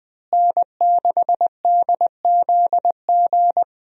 Morse code, heard as a keyed tone, about 20 wpm, 700 Hz. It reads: N6DZG
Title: USA Call Signs - 5 Characters - Repeated 3x - 20wpm